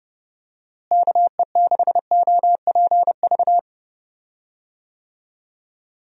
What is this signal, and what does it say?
Morse code, heard as a keyed tone, about 30 wpm, 700 Hz. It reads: KE6OPV
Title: USA Call Signs - 6 Characters - Repeated 2x - 30wpm